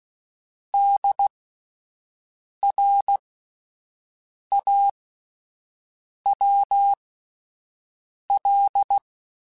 Morse code, heard as a keyed tone, about 16 wpm, spaced as for 5 wpm, 800 Hz. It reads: DRAWL